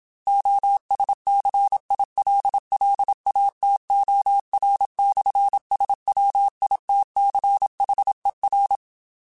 Morse code, heard as a keyed tone, about 27 wpm, 800 Hz. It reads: OSCILLATOR/SWITCHER
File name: Technical Subjects / Technical Subjects-mp3s / Sw Pwr Sup @25